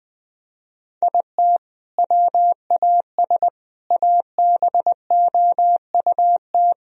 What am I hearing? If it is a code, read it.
Morse code, heard as a keyed tone, about 20 wpm, 700 Hz. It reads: IT WAS ABOUT
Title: Sentences from Top 1000 Words - Review Part 3 - 20wpm